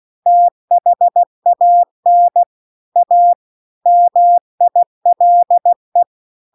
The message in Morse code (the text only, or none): THAN A MILE